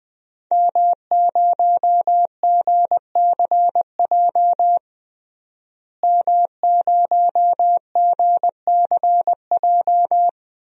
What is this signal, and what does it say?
Morse code, heard as a keyed tone, about 20 wpm, 700 Hz. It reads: M0GCJ M0GCJ
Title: DXCC-100 Call Signs - 5 Characters - Repeated 2x - 20wpm